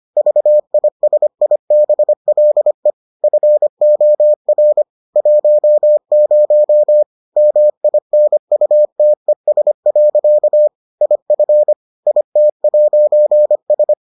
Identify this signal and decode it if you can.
Morse code, heard as a keyed tone, about 25 wpm, 600 Hz. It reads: VISIBLE FOR 10 MINUTES. IF IT'S